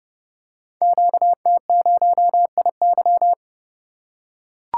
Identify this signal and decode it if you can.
Morse code, heard as a keyed tone, about 30 wpm, 700 Hz. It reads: QT0IY
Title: Code Groups - 5 Letter:Numbers - 30wpm